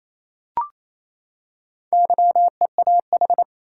Morse code, heard as a keyed tone, about 28 wpm, 700 Hz. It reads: YEAH
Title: Four Letter Words - 28wpm